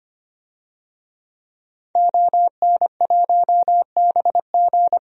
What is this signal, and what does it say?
Morse code, heard as a keyed tone, about 25 wpm, 700 Hz. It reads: ON1BG
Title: DXCC-100 Call Signs - 5 Characters - Repeated 3x - 25wpm